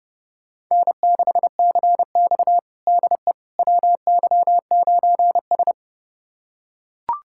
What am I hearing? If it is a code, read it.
Morse code, heard as a keyed tone, about 32 wpm, 700 Hz. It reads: N6CX DE WY9S
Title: USA Callsign DE Callsign - 4 Characters - 30wpm